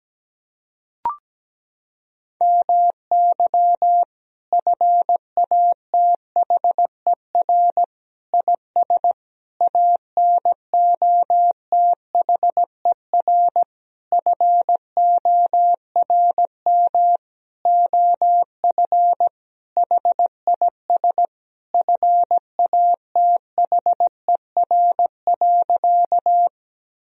Morse code, heard as a keyed tone, about 17 wpm, 700 Hz. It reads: MY FATHER IS ANOTHER FORM OF HIS FATHER.
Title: Sentences from Top 1000 Words - Review Part 1 - 17wpm